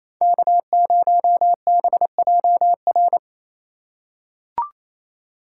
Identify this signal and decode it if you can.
Morse code, heard as a keyed tone, about 28 wpm, 700 Hz. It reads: K0BJR
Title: USA Call Signs - 5 Characters - 2x Character Spacing - 28wpm